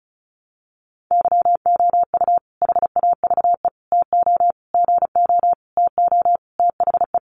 Morse code, heard as a keyed tone, about 34 wpm, 700 Hz. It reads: YOU HAVE TO GO TO THE
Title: Sentences from Top 1300 Words 35wpm